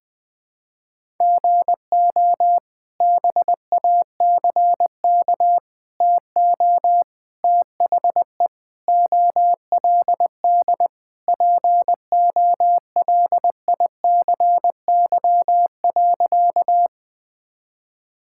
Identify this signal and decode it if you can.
Morse code, heard as a keyed tone, about 21 wpm, 700 Hz. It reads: GO BACK TO THE OLD POLICY.
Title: Sentences from Top 1000 Words - Review Part 5 - 20wpm